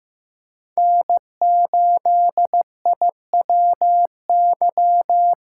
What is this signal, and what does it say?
Morse code, heard as a keyed tone, about 15 wpm, 700 Hz. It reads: N8IWY
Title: K1USN SST QSO Practice - 15wpm